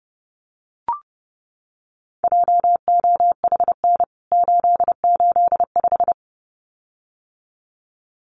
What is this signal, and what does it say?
Morse code, heard as a keyed tone, about 30 wpm, 700 Hz. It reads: JOHN 885
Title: CWT Exchange 30wpm